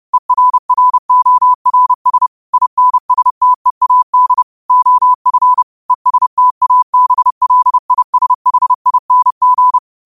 Morse code, heard as a keyed tone, about 30 wpm, 1000 Hz. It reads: ERRORS INSTEAD OF ESTABLISHING